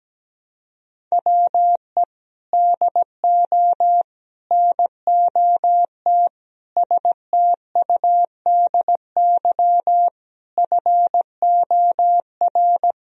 Morse code, heard as a keyed tone, about 17 wpm, 700 Hz. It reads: WE DO NOT STUDY FOR
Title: Sentences from Top 1000 Words - Review Part 1 - 17wpm